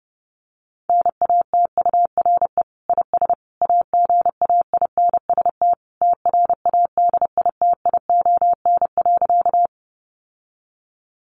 Morse code, heard as a keyed tone, about 30 wpm, 700 Hz. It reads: NATURE IS AGAINST TRADITION.
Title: Sentences from Top 1500 Words 30wpm